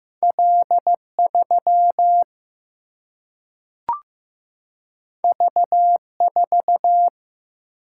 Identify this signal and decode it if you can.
Morse code, heard as a keyed tone, about 15 wpm, 700 Hz. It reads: L 3 V 4
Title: Single-Letter Single-Number 15wpm